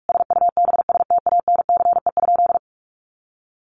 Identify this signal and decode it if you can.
Morse code, heard as a keyed tone, about 44 wpm, 700 Hz. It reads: SUBSTANCE?
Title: Sentences from Top 1500 Words 45wpm